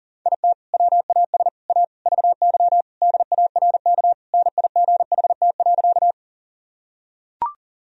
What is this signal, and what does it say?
Morse code, heard as a keyed tone, about 40 wpm, 700 Hz. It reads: IT WAS A VY DARK NIGHT.
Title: Sentences from Top 500 Words and Abbreviations 40wpm